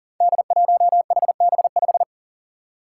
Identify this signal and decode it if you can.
Morse code, heard as a keyed tone, about 40 wpm, 700 Hz. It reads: D1HB5
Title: Five Challenging Letter-Number Combinations 40wpm